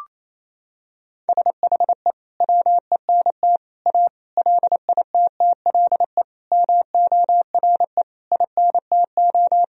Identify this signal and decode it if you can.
Morse code, heard as a keyed tone, about 28 wpm, 690 Hz. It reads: SHE WENT A LITTLE MORE INTO
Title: Sentences from Top 1000 Words - Review Part 2 - 28wpm